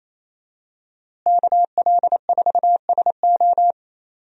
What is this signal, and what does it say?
Morse code, heard as a keyed tone, about 28 wpm, 700 Hz. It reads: KL4SO
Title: USA Call Signs - 5 Characters - Repeated 3x - 28wpm